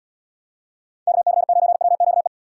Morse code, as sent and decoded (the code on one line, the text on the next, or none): -.- -.-. ----. .-- --.- .
KC9WQE